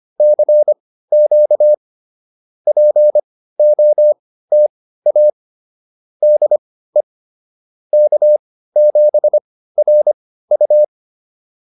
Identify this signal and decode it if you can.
Morse code, heard as a keyed tone, about 25 wpm, spaced as for 15 wpm, 590 Hz. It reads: CQ POTA DE K7RU
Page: intermediate-practice-files